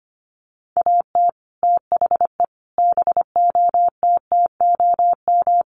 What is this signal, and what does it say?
Morse code, heard as a keyed tone, about 25 wpm, 700 Hz. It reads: AT THE BOTTOM
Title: Sets of 5 Words 25wpm